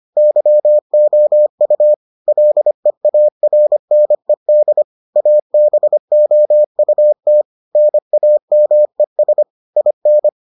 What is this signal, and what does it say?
Morse code, heard as a keyed tone, about 25 wpm, 600 Hz. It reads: YOU LEARNED ABOUT NAMES IN